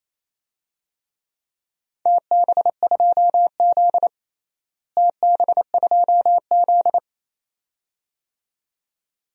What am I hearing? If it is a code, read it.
Morse code, heard as a keyed tone, about 28 wpm, 700 Hz. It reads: TB2Z TB2Z
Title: DXCC-100 Call Signs - 4 Characters - Repeated 2x - 28wpm